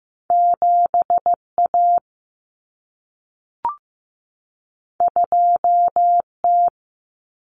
Morse code, heard as a keyed tone, about 15 wpm, 700 Hz. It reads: 7A 2T
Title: Common Cut Numbers - 2-Digit 15wpm